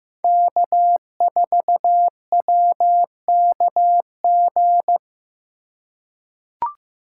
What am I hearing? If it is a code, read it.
Morse code, heard as a keyed tone, about 15 wpm, 700 Hz. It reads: K4WKG
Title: USA Call Signs - 2x Character Spacing - 15wpm